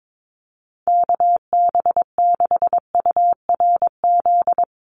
Morse code, heard as a keyed tone, about 22 wpm, 700 Hz. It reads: KB6URZ
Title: USA Call Signs Repeated 3x 22wpm